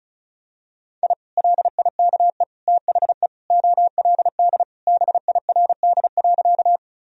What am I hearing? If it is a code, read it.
Morse code, heard as a keyed tone, about 34 wpm, 700 Hz. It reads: I LIKE THE OLD BIRD.
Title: Sentences from Top 1000 Words - Review Part 2 - 35wpm